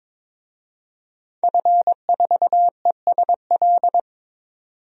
Morse code, heard as a keyed tone, about 22 wpm, 700 Hz. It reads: F4ESL
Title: DXCC-100 Call Signs - 5 Characters - 22wpm